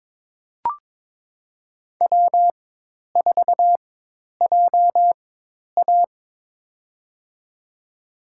Morse code, heard as a keyed tone, about 21 wpm, spaced as for 10 wpm, 700 Hz. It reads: W4JA
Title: USA Call Signs - 4x Character Spacing - 22wpm